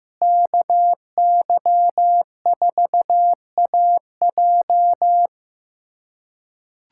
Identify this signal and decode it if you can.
Morse code, heard as a keyed tone, about 15 wpm, 700 Hz. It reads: KY4AJ
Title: USA Callsign DE Callsign - 5 Characters - 15wpm